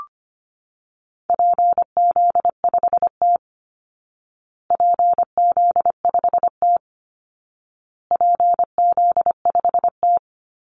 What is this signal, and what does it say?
Morse code, heard as a keyed tone, about 25 wpm, 700 Hz. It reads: PZ5T PZ5T PZ5T
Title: Call Signs Repeated 2x - 25wpm